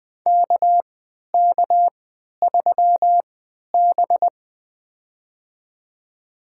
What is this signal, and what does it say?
Morse code, heard as a keyed tone, about 20 wpm, spaced as for 11 wpm, 700 Hz. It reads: KK3B
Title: USA Call Signs - 4 Characters - 3x Character Spacing - 20wpm